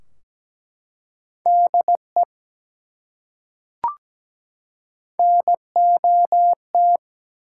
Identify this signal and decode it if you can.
Morse code, heard as a keyed tone, about 17 wpm, 700 Hz. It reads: DE NOT
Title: Top 180 QSO Elements 17wpm